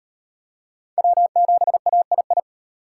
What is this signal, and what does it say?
Morse code, heard as a keyed tone, about 38 wpm, 700 Hz. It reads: W7AII